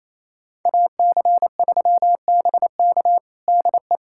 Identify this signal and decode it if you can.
Morse code, heard as a keyed tone, about 27 wpm, 700 Hz. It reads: AC3BK DE
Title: USA Callsign DE Callsign - 5 Characters - 28wpm